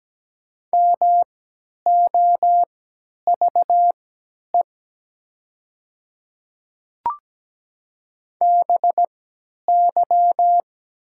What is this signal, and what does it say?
Morse code, heard as a keyed tone, about 17 wpm, spaced as for 10 wpm, 700 Hz. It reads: MOVE BY
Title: 3x Character Spacing - Top 300 Words - 17wpm